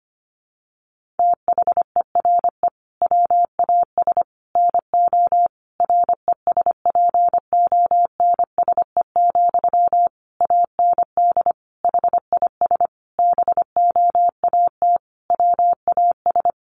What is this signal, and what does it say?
Morse code, heard as a keyed tone, about 25 wpm, 700 Hz. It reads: THERE WAS NO RESPONSE, AND HIS BOAT WAS
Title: Sentences from Top 1400 Words 25wpm